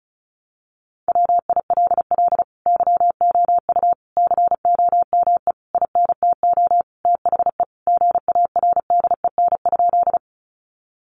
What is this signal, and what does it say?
Morse code, heard as a keyed tone, about 34 wpm, 700 Hz. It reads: WILL YOU COME INTO THE GARDEN?
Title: Sentences from Top 700 Words 35wpm